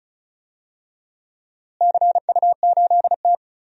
Morse code, heard as a keyed tone, about 36 wpm, 700 Hz. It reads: CU8T
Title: DXCC-100 Call Signs - 4 Characters - Repeated 2x - 35wpm